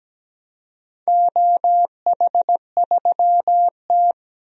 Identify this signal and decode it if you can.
Morse code, heard as a keyed tone, about 17 wpm, 700 Hz. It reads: OH3T
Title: DXCC-100 Call Signs - 4 Characters - 17wpm